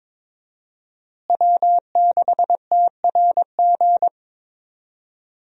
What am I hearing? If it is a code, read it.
Morse code, heard as a keyed tone, about 23 wpm, 700 Hz. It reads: W6TRG
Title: USA Call Signs - 5 Characters - Repeated 2x - 22wpm